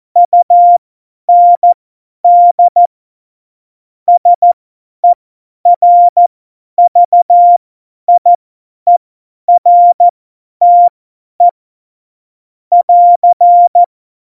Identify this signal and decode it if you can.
Morse code, heard as a keyed tone, about 14 wpm, spaced as for 10 wpm, 700 Hz. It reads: UND SERVIERTE <AR>
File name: Morsekurs / Witze / Witz 26. Juni 2018, 50 F70